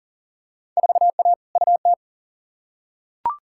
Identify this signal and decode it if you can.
Morse code, heard as a keyed tone, about 40 wpm, 700 Hz. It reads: 4A UT
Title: Field Day Exchanges - 40wpm